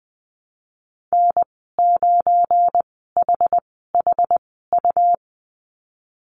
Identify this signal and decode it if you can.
Morse code, heard as a keyed tone, about 20 wpm, spaced as for 15 wpm, 700 Hz. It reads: N9HHU
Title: 193 Lesson 31g - Letter v - Call Signs 20wpm